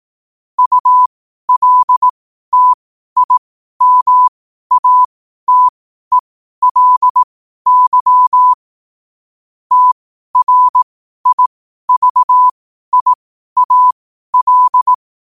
Morse code, heard as a keyed tone, about 18 wpm, spaced as for 12 wpm, 1000 Hz. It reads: ULTIMATELY TRIVIAL
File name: qotdfiles / 12wpm